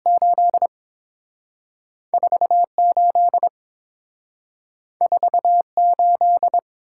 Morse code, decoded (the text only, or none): 84 84 8